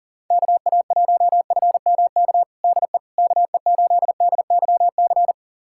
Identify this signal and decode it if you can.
Morse code, heard as a keyed tone, about 40 wpm, 700 Hz. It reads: KA1FMK DE KE8DYC
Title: USA Callsign DE Callsign - 6 Characters - 40wpm